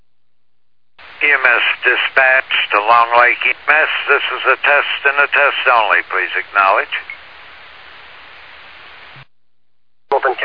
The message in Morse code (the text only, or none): none